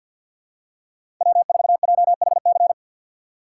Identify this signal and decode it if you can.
Morse code, heard as a keyed tone, about 53 wpm, 700 Hz. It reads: W4JHC
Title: USA Call Signs - 6x Character Spacing - 50wpm